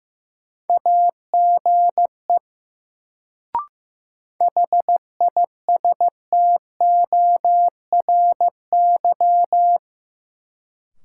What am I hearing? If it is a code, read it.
Morse code, heard as a keyed tone, about 15 wpm, 700 Hz. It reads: AGE HISTORY
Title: Sets of 3 Words - New Word Drill - 15wpm